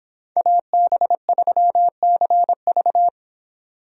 Morse code, heard as a keyed tone, about 29 wpm, 700 Hz. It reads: AB3CV